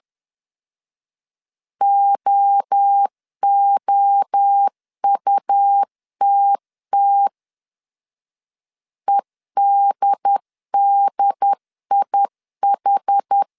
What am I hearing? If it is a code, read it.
Morse code, heard as a keyed tone, about 10 wpm, 800 Hz. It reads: OOUTT EDDIH
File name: 10wpm